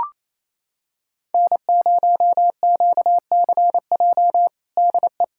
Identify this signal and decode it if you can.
Morse code, heard as a keyed tone, about 28 wpm, 700 Hz. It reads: N0QCJ DE
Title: Practice QSO - Exchange 4 - Ending - 28wpm